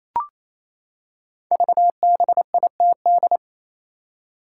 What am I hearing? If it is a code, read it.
Morse code, heard as a keyed tone, about 29 wpm, 700 Hz. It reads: VBITD
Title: Code Groups - 5 Letters - 28wpm